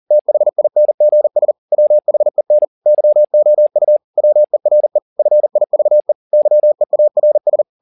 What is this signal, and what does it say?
Morse code, heard as a keyed tone, about 40 wpm, 600 Hz. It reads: THINGS WHEN YOU WERE FIVE YEARS